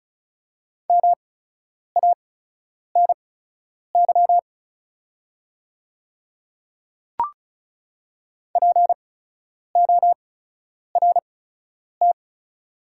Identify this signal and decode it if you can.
Morse code, heard as a keyed tone, about 36 wpm, spaced as for 10 wpm, 690 Hz. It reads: MANY PORT
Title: 8x Character Spacing - Top 200 Words - 35wpm